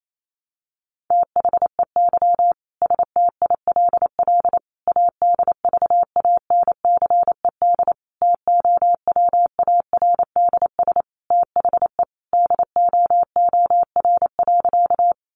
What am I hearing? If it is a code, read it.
Morse code, heard as a keyed tone, about 28 wpm, 700 Hz. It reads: THEY STILL ADVANCED TOWARDS THE DOOR.
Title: Sentences from Top 1200 Words 28wpm